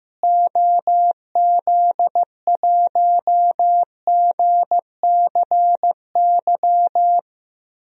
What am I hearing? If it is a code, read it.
Morse code, heard as a keyed tone, about 15 wpm, 700 Hz. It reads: OZ1GCY
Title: DXCC-100 Call Signs - 6 Characters - Repeated 3x - 15wpm